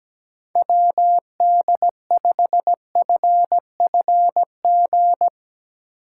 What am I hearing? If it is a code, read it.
Morse code, heard as a keyed tone, about 17 wpm, 700 Hz. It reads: WD5FFG
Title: USA Call Signs - 6 Characters - 3x Character Spacing - 17wpm